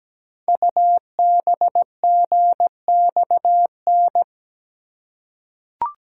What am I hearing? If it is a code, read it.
Morse code, heard as a keyed tone, about 17 wpm, 700 Hz. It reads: UBGXN
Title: Code Groups - 5 Letters - 17wpm